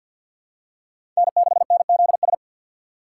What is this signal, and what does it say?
Morse code, heard as a keyed tone, about 50 wpm, 700 Hz. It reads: N6NZS
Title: USA Call Signs - 6x Character Spacing - 50wpm